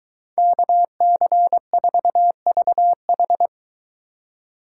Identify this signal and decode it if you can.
Morse code, heard as a keyed tone, about 21 wpm, 700 Hz. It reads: KC4VH